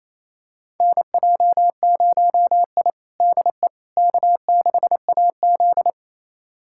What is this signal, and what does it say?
Morse code, heard as a keyed tone, about 27 wpm, 700 Hz. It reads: NJ0I DE K6AZ
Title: USA Callsign DE Callsign - 4 Characters - 28wpm